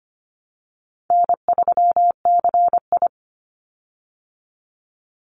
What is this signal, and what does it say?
Morse code, heard as a keyed tone, about 25 wpm, 700 Hz. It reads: N3CI
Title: Call Signs Repeated 3x - 25wpm